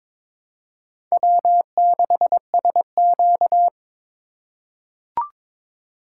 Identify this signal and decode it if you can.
Morse code, heard as a keyed tone, about 22 wpm, 700 Hz. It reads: W6SQ
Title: USA Call Signs - 4 Characters - 4x Character Spacing - 22wpm